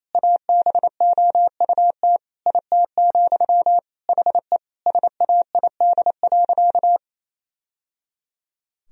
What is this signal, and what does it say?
Morse code, heard as a keyed tone, about 28 wpm, 700 Hz. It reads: ABOUT IT, HE SAID.
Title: Sentences from Top 1000 Words - Review Part 3 - 28wpm